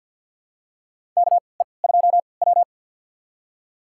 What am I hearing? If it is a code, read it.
Morse code, heard as a keyed tone, about 46 wpm, spaced as for 27 wpm, 700 Hz. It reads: KE2W
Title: USA Call Signs - 3x Character Spacing - 50wpm